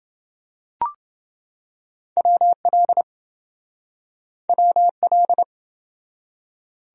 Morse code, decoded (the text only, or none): WL WL